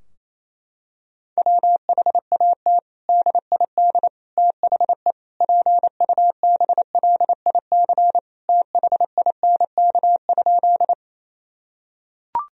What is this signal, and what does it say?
Morse code, heard as a keyed tone, about 28 wpm, 700 Hz. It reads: WHAT DID THE PUBLIC THINK?
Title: Sentences from Top 1000 Words - Review Part 3 - 28wpm